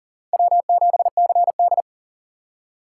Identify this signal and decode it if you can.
Morse code, heard as a keyed tone, about 52 wpm, 700 Hz. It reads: W7CD